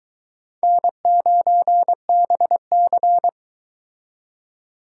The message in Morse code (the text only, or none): N9BC